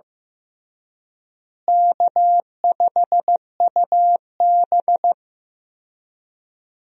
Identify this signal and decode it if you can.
Morse code, heard as a keyed tone, about 15 wpm, 700 Hz. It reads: K5UB
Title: USA Call Signs - 4 Characters - Repeated 2x - 15wpm